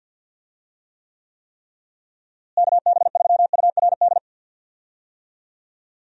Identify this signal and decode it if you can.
Morse code, heard as a keyed tone, about 48 wpm, 700 Hz. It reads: KB3URD
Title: USA Call Signs - 6 Characters - Repeated 3x - 50wpm